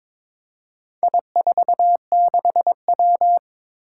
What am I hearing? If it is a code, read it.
Morse code, heard as a keyed tone, about 22 wpm, 700 Hz. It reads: I46W
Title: Four Challenging Letter-Number Combinations 22wpm